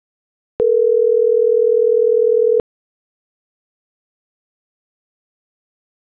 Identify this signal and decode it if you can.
Ringback tone (North America), 440 + 480 Hz.